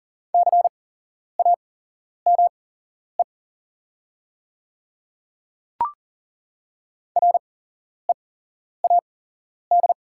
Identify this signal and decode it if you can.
Morse code, heard as a keyed tone, about 41 wpm, spaced as for 11 wpm, 690 Hz. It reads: CAME READ
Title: 8x Character Spacing - Top 300 Words - 40wpm